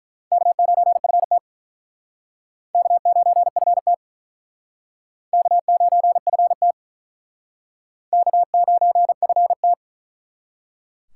Morse code, heard as a keyed tone, about 47 wpm, 700 Hz. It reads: K9FT K9FT K9FT K9FT